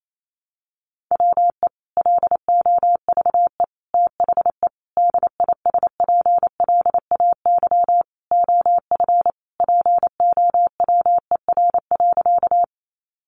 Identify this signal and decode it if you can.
Morse code, heard as a keyed tone, about 28 wpm, 700 Hz. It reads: WE LOVE THE DISPLAY OF POWER.
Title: Sentences from Top 1200 Words 28wpm